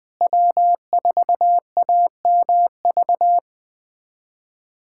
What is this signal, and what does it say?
Morse code, heard as a keyed tone, about 20 wpm, 700 Hz. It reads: W4AMV
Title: USA Callsign DE Callsign 20wpm